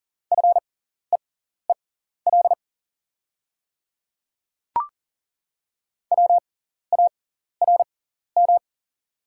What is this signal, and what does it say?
Morse code, heard as a keyed tone, about 38 wpm, spaced as for 14 wpm, 700 Hz. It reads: FEEL WARM
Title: 6x Character Spacing - Top 500 Words - 40wpm